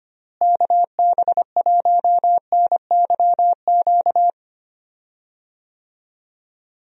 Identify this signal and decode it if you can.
Morse code, heard as a keyed tone, about 25 wpm, 700 Hz. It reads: KB1NYQ
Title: USA Call Signs - 6 Characters - Repeated 2x - 25wpm